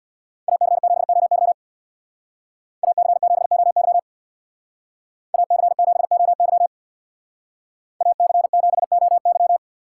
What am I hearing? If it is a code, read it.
Morse code, heard as a keyed tone, about 61 wpm, 700 Hz. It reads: AC7OY AC7OY AC7OY AC7OY
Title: Speed Racing - USA 2x2 Call Signs 50wpm